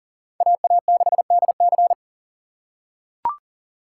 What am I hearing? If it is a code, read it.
Morse code, heard as a keyed tone, about 40 wpm, 700 Hz. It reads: AA6DC
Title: USA Call Signs - 6x Character Spacing - 40wpm